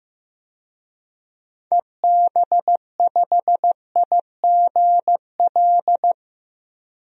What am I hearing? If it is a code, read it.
Morse code, heard as a keyed tone, about 16 wpm, 700 Hz. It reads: EB5IGL
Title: DXCC-100 Call Signs - 6 Characters - Repeated 3x - 15wpm